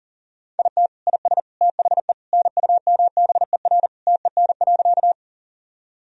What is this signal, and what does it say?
Morse code, heard as a keyed tone, about 40 wpm, 700 Hz. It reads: IT IS THE NUMBER TEN.
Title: Sentences from Top 1000 Words - Review Part 2 - 40wpm